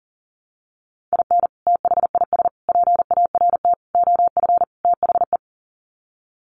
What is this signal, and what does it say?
Morse code, heard as a keyed tone, about 40 wpm, 700 Hz. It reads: IN THIS PART OF THE
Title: Sets of 5 Words 40wpm